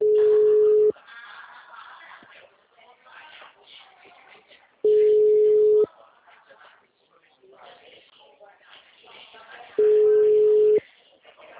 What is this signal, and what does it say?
Ringback tone (Europe, 425 Hz), with speech.
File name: out- -2017- - - .69036